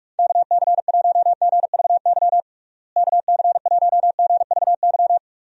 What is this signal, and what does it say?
Morse code, heard as a keyed tone, about 42 wpm, 700 Hz. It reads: KC1GVY KC1GVY
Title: USA Call Signs - 6 Characters - Repeated 2x - 45wpm